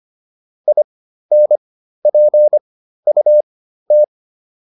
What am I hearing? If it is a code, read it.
Morse code, heard as a keyed tone, about 25 wpm, spaced as for 13 wpm, 600 Hz. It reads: INPUT